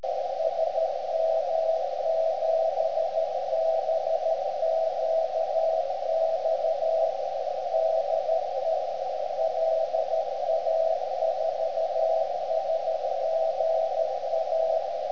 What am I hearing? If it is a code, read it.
Morse code, heard as a keyed tone, about 13 wpm, 650 Hz, with radio noise. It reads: SM7BQX DE LA4SGA
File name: ft8-vs-cw / 75-to-150 CW 300hz